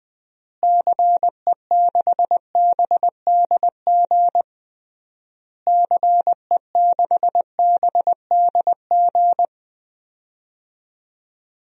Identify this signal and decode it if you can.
Morse code, heard as a keyed tone, about 20 wpm, 700 Hz. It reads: CE6BDG CE6BDG
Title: DXCC-100 Call Signs - 6 Characters - Repeated 3x - 20wpm